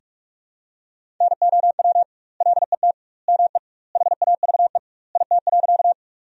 Morse code, heard as a keyed tone, about 41 wpm, 700 Hz. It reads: NOW LET ME HAVE IT.
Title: 2x Word-Spacing - Sentences from Top 300 Words - 45wpm